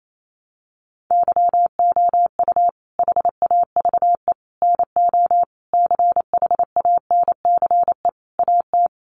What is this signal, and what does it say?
Morse code, heard as a keyed tone, about 28 wpm, 700 Hz. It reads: YOU HAVE NO CHANCE AT
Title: Sentences from Top 800 Words 28wpm